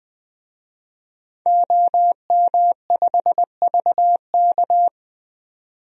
Morse code, heard as a keyed tone, about 20 wpm, 700 Hz. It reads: OM5VK
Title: DXCC-100 Call Signs - 5 Characters - 20wpm